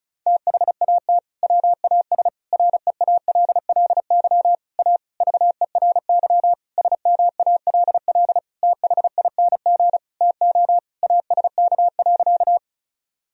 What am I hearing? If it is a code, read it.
Morse code, heard as a keyed tone, about 35 wpm, 700 Hz. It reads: THAT WAS REALLY A VERY SMALL THING TO ASK.
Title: Sentences from Top 1000 Words - Review Part 1 - 35wpm